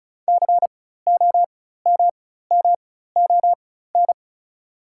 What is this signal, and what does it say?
Morse code, heard as a keyed tone, about 36 wpm, spaced as for 16 wpm, 690 Hz. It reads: COMMON
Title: 4x Character Spacing - Top 500 Words - 35wpm